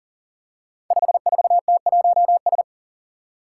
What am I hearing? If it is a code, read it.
Morse code, heard as a keyed tone, about 39 wpm, 700 Hz. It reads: 54T1S
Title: Five Challenging Letter-Number Combinations 40wpm